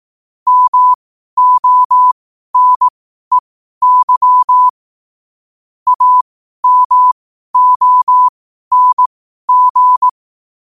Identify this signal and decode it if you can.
Morse code, heard as a keyed tone, about 18 wpm, spaced as for 12 wpm, 1000 Hz. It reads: MONEY AMONG
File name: qotdfiles / 12wpm